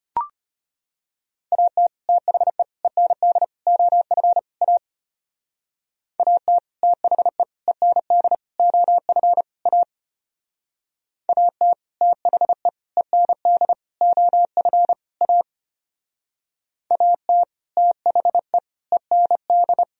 Morse code, heard as a keyed tone, about 30 wpm, 700 Hz. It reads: AT THE END OF A AT THE END OF A AT THE END OF A AT THE END
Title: Speed-Racing - Sets of 5 Words 25wpm